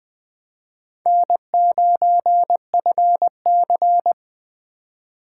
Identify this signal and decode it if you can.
Morse code, heard as a keyed tone, about 20 wpm, 700 Hz. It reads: N9FC
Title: Speed Racing - USA Call Signs 17wpm